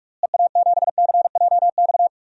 Morse code, heard as a keyed tone, about 44 wpm, 700 Hz. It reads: EA7CJX
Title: DXCC-100 Call Signs - 6 Characters - Repeated 3x - 45wpm